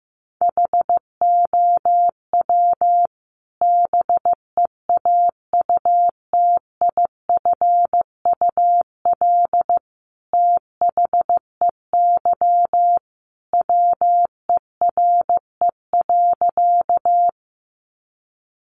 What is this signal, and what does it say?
Morse code, heard as a keyed tone, about 15 wpm, 700 Hz. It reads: HOW BEAUTIFUL THEY WERE.
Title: Sentences from Top 900 Words 15wpm